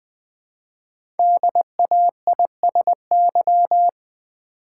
Morse code, heard as a keyed tone, about 20 wpm, 700 Hz. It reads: DAISY
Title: Easy Words with Hints - 20wpm